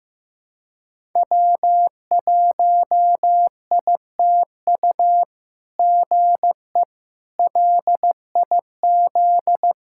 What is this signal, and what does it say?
Morse code, heard as a keyed tone, about 15 wpm, 700 Hz. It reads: W1ITU GE LIZ
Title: SOTA QSO Practice 15wpm